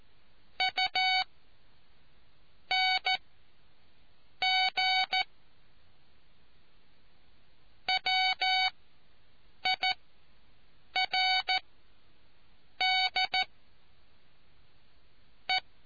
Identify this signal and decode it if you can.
Morse code, heard as a keyed tone, about 14 wpm, spaced as for 6 wpm, 750 Hz. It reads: UNG WIRD E